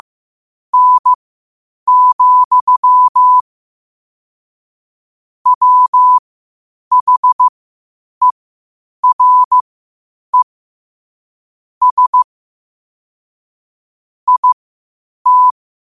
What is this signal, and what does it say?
Morse code, heard as a keyed tone, about 15 wpm, spaced as for 8 wpm, 1000 Hz. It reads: N, WHERE S IT
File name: qotdfiles / 8wpm